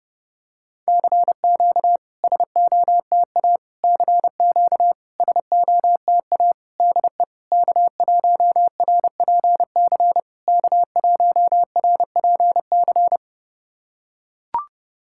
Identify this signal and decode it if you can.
Morse code, heard as a keyed tone, about 30 wpm, 700 Hz. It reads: CQ SOTA CQ SOTA DE K1RPC K1RPC
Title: SOTA QSO Practice 30wpm